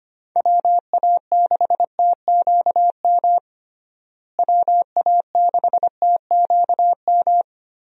Morse code, heard as a keyed tone, about 25 wpm, 700 Hz. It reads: WA6TQM WA6TQM
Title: USA Call Signs - 6 Characters - Repeated 2x - 25wpm